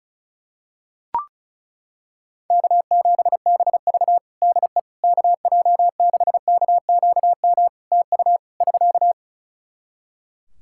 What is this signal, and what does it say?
Morse code, heard as a keyed tone, about 34 wpm, 700 Hz. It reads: K7BV DE KJ6KQM TU <SK>
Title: Practice QSO - Exchange 4 - Ending - 35wpm